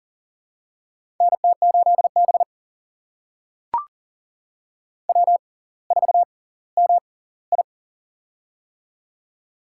Morse code, heard as a keyed tone, about 40 wpm, spaced as for 20 wpm, 700 Hz. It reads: NT8B W4MI
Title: USA Call Signs - 4 Characters - 6x Character Spacing - 40wpm